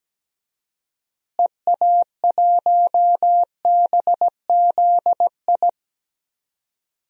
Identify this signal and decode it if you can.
Morse code, heard as a keyed tone, about 17 wpm, 700 Hz. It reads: EA1BZI
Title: DXCC-100 Call Signs - 6 Characters - Repeated 3x - 17wpm